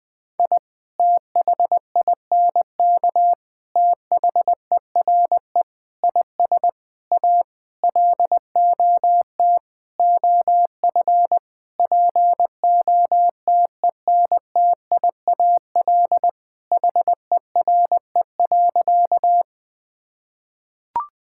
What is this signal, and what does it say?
Morse code, heard as a keyed tone, about 21 wpm, 700 Hz. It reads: I THINK THERE IS A LOT OF POTENTIAL HERE.
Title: Sentences with 6-Word Phrases - Spoken Beginning - 20wpm